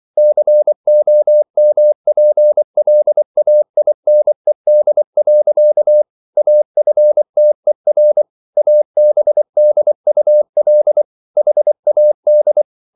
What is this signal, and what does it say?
Morse code, heard as a keyed tone, about 24 wpm, 600 Hz. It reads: COMPLAINED. AFTER ABDUL HAD